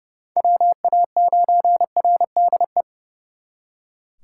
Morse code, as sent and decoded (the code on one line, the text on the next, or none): .-- .- ----. .-. -.. .
WA9RDE